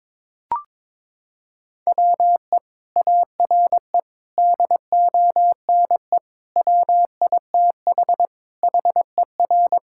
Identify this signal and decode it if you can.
Morse code, heard as a keyed tone, about 22 wpm, 700 Hz. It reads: WE ARE DONE WITH HER
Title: Sentences from Top 1600 Words 22wpm